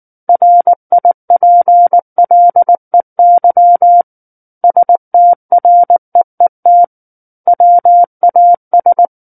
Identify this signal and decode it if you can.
Morse code, heard as a keyed tone, about 19 wpm, 700 Hz. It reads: RIPLEY STREET WAS
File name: Book One / 10-Chapter 10 - 19 WPM